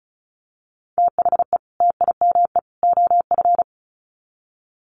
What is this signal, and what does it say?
Morse code, heard as a keyed tone, about 34 wpm, 700 Hz. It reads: THE TIME OF
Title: Sets of 3 Words 35wpm